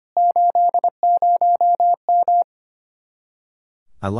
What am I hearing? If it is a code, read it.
Morse code, heard as a keyed tone, about 25 wpm, 700 Hz. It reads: 80M
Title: Speed-Racing Common QSO Phrases Encore - 25wpm